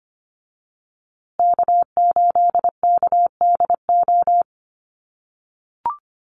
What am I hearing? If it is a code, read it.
Morse code, heard as a keyed tone, about 25 wpm, 700 Hz. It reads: K8KDO
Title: USA Call Signs Repeated 2x 25wpm